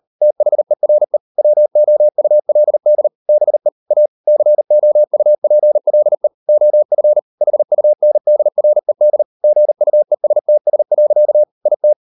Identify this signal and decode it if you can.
Morse code, heard as a keyed tone, about 38 wpm, 600 Hz. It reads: THERE WOULD BE A COUPLE OF HUNDRED GUESTS. IT